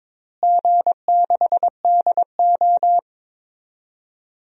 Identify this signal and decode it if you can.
Morse code, heard as a keyed tone, about 22 wpm, 700 Hz. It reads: G6DO
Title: DXCC-100 Call Signs - 4 Characters - Repeated 3x - 22wpm